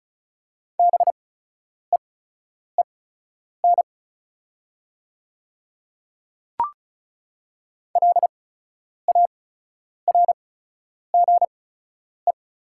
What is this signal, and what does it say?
Morse code, heard as a keyed tone, about 36 wpm, spaced as for 10 wpm, 690 Hz. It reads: BEEN LARGE
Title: 8x Character Spacing - Top 200 Words - 35wpm